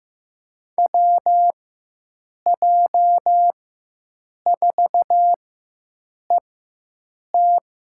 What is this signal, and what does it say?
Morse code, heard as a keyed tone, about 16 wpm, spaced as for 7 wpm, 700 Hz. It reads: WJ4ET
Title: USA Call Signs - 5 Characters - 4x Character Spacing - 15wpm